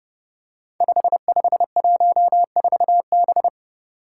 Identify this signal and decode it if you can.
Morse code, heard as a keyed tone, about 30 wpm, 700 Hz. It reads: 5514B
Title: Five Challenging Letter-Number Combinations 30wpm